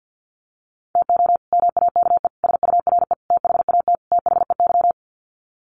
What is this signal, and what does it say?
Morse code, heard as a keyed tone, about 50 wpm, 700 Hz. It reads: TO MAKE SURE THAT THEY
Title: Sets of 5 Words Encore2 50wpm